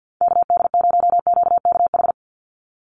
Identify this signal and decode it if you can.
Morse code, heard as a keyed tone, about 50 wpm, 700 Hz. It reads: KD0QKH
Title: USA Call Signs 50wpm